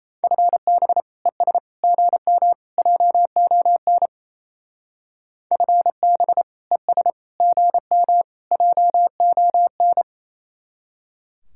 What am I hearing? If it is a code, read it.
Morse code, heard as a keyed tone, about 32 wpm, 700 Hz. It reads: FB ES GM JON FB ES GM JON
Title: Speed-Racing Common QSO Phrases - 28wpm